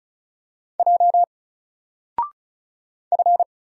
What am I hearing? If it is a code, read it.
Morse code, heard as a keyed tone, about 35 wpm, 700 Hz. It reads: J F
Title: Single Character (Letter:Number:Punctuation:Prosign) 35wpm